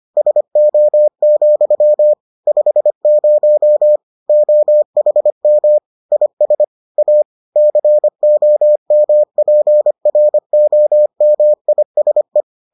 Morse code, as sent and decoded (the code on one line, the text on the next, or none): ... --- --..-- / ..... ----- / --- .... -- / .. ... / .- / -.-. --- -- .--. .-. --- -- .. ... .
SO, 50 OHM IS A COMPROMISE